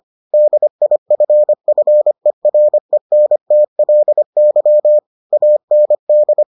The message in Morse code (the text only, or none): DIFFERENTLY AND